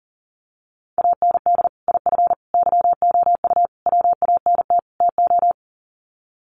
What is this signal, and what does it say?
Morse code, heard as a keyed tone, about 40 wpm, 700 Hz. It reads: AND IF YOU WANT TO
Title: Sets of 5 Words Encore 40wpm